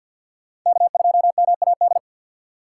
Morse code, heard as a keyed tone, about 46 wpm, 700 Hz. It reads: K2MAD
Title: USA Call Signs - 3x Character Spacing - 50wpm